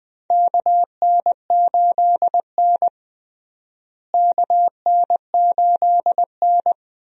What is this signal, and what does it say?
Morse code, heard as a keyed tone, about 20 wpm, 700 Hz. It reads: KN8N KN8N
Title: USA Call Signs - 4 Characters - Repeated 2x - 20wpm